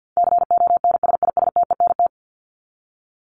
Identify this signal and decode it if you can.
Morse code, heard as a keyed tone, about 50 wpm, 700 Hz. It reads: CONSISTENT
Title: Top 1251-1500 Words 50wpm